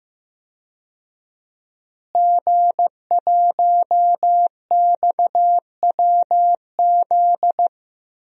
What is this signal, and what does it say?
Morse code, heard as a keyed tone, about 16 wpm, 700 Hz. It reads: G1XWZ
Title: DXCC-100 Call Signs - 5 Characters - Repeated 3x - 15wpm